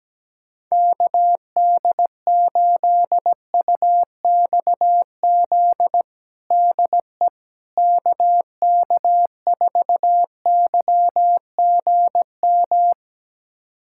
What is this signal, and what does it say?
Morse code, heard as a keyed tone, about 17 wpm, 700 Hz. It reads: KD8UXZ DE KK4YGM
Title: USA Callsign DE Callsign - 6 Characters - 17wpm